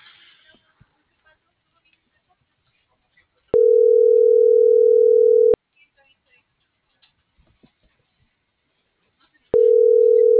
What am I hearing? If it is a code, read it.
Ringback tone (North America), 440 + 480 Hz.